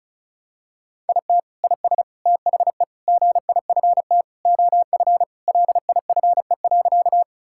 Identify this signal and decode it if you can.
Morse code, heard as a keyed tone, about 36 wpm, 700 Hz. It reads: IT IS THE GIFT OF LIFE.
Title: Sentences from Top 1000 Words - Review Part 6 - 35wpm